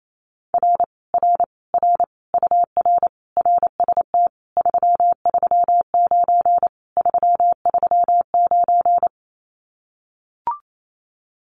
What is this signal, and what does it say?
Morse code, heard as a keyed tone, about 28 wpm, 700 Hz. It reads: R R R UR RST 339 339
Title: Common QSO Phrases Encore 28wpm